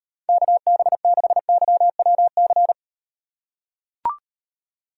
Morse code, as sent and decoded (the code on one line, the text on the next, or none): -.- -... -.... -.-- .-- -.-.
KB6YWC